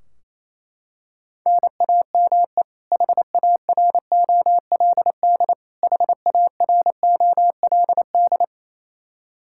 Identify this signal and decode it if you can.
Morse code, heard as a keyed tone, about 28 wpm, 700 Hz. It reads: NAME HAROLD HAROLD
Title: Practice QSO - Exchange 1 - Beginning - 28wpm